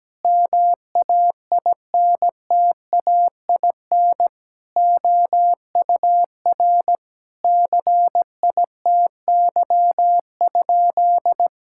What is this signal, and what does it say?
Morse code, heard as a keyed tone, about 17 wpm, 700 Hz. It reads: MAINTAIN OUR CITY?
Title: Sentences from Top 1000 Words - Review Part 5 - 17wpm